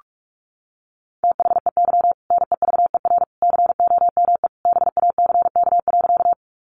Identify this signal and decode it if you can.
Morse code, heard as a keyed tone, about 44 wpm, 700 Hz. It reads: THEY NEVER COME BACK.
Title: Sentences from Top 300 Words 45wpm